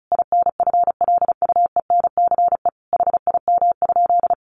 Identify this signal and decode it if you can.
Morse code, heard as a keyed tone, about 34 wpm, 700 Hz. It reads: INFLUENCE HIM?
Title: Sentences from Top 1100 Words 35wpm